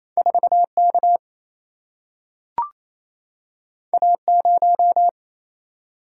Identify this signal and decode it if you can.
Morse code, heard as a keyed tone, about 28 wpm, 700 Hz. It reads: 4K A0
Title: Single-Letter Single-Number 28wpm